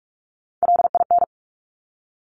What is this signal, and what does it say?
Morse code, heard as a keyed tone, about 46 wpm, 700 Hz. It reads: LIN
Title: Top Three Letter Combinations in Words 45wpm